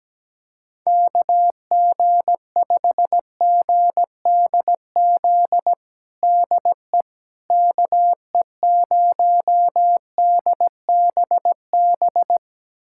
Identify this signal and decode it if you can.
Morse code, heard as a keyed tone, about 17 wpm, 700 Hz. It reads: KG5GDZ DE KE0DBB
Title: USA Callsign DE Callsign - 6 Characters - 17wpm